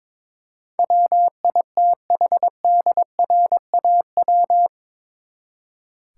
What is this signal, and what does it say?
Morse code, heard as a keyed tone, about 22 wpm, 700 Hz. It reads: WITHDRAW
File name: practice / 22wpm / Eight Letter Words - 22wpm